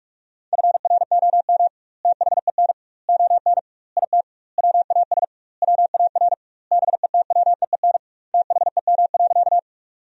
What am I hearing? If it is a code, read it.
Morse code, heard as a keyed tone, about 45 wpm, 700 Hz. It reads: FROM THEN ON IT WAS WAR BETWEEN THEM.
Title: 2x Word-Spacing - Sentences from Top 400 Words - 45wpm